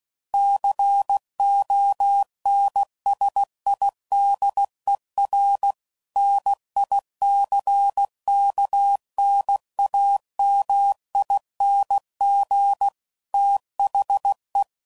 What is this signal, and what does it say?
Morse code, heard as a keyed tone, about 16 wpm, 800 Hz. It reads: CONSIDER NICKNAMING THE
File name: CW Code Groups @15WPM / Technical Subjects / AND Gate